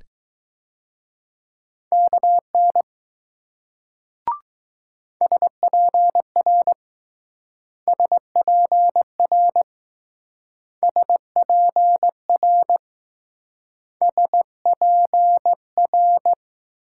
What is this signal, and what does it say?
Morse code, heard as a keyed tone, about 20 wpm, 700 Hz. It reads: KN SPR SPR SPR S P R